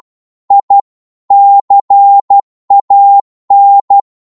Morse code, heard as a keyed tone, about 12 wpm, 800 Hz. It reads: I CAN